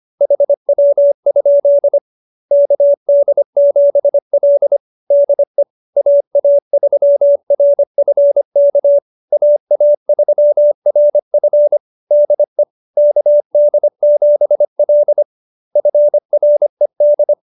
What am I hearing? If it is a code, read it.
Morse code, heard as a keyed tone, about 25 wpm, 590 Hz. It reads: HW? KD7L DE AA3RF K AA3RF DE KD7L FRED